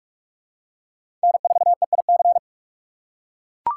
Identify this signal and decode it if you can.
Morse code, heard as a keyed tone, about 44 wpm, 700 Hz. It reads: N4EIC
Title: USA Call Signs - 5 Characters - 3x Character Spacing - 45wpm